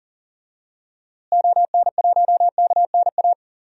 Morse code, heard as a keyed tone, about 37 wpm, 700 Hz. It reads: ON1KNA
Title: DXCC-100 Call Signs - 6 Characters - Repeated 3x - 40wpm